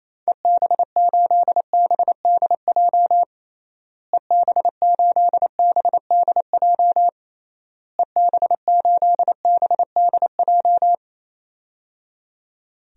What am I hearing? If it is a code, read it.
Morse code, heard as a keyed tone, about 28 wpm, 700 Hz. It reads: EB8BDJ EB8BDJ EB8BDJ
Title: DXCC-100 Call Signs - 6 Characters - Repeated 3x - 28wpm